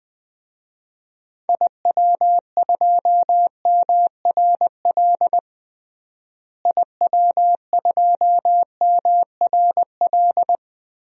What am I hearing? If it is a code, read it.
Morse code, heard as a keyed tone, about 20 wpm, 700 Hz. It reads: IW2MRL IW2MRL
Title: DXCC-100 Call Signs - 6 Characters - Repeated 3x - 20wpm